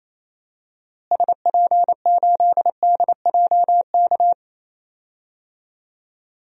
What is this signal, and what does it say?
Morse code, heard as a keyed tone, about 28 wpm, 700 Hz. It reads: SP8DJK
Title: DXCC-100 Call Signs - 6 Characters - Repeated 3x - 28wpm